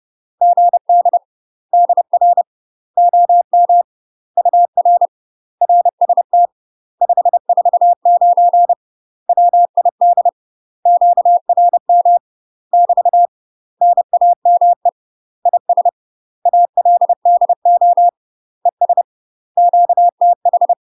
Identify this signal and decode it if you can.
Morse code, heard as a keyed tone, about 30 wpm, 700 Hz. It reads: GD DR OM UR RST 549 WID QRM <BT> NAME IS ALDO ES QTH
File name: UK MASTER QSOs TRAINING / 30WPM / UK MASTER QSOs 0026